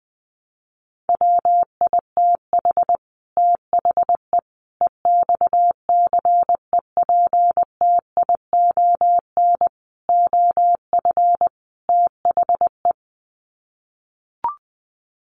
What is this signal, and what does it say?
Morse code, heard as a keyed tone, about 20 wpm, 700 Hz. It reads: WITH THE EXCEPTION OF THE
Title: Sets of 5 Words Encore 20wpm